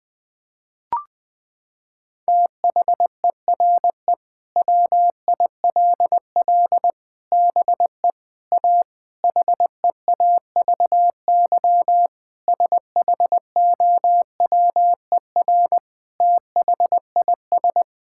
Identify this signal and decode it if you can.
Morse code, heard as a keyed tone, about 20 wpm, 700 Hz. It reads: THERE WILL BE A HEAVY SHOWER THIS
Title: Sentences from Top 1600 Words 20wpm